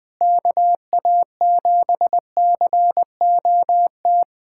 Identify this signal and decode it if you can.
Morse code, heard as a keyed tone, about 21 wpm, 700 Hz. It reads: KA7COT
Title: USA Callsign DE Callsign 20wpm